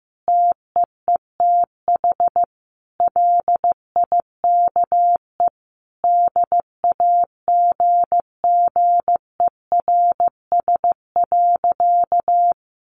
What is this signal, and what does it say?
Morse code, heard as a keyed tone, about 15 wpm, 700 Hz. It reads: TEETH LIKE DAGGERS.
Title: A Sound Of Thunder 15wpm